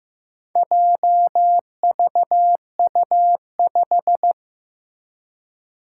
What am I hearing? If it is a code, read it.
Morse code, heard as a keyed tone, about 15 wpm, 700 Hz. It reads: JVU5
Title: Four Challenging Letter-Number Combinations 15wpm